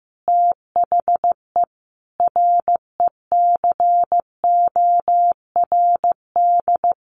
Morse code, heard as a keyed tone, about 15 wpm, 700 Hz. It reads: THE RECORD
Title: Sentences from Top 500 Words 15wpm